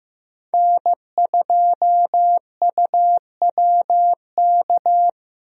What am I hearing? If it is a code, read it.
Morse code, heard as a keyed tone, about 16 wpm, 700 Hz. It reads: N2UWK
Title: USA Call Signs - 5 Characters - Repeated 3x - 15wpm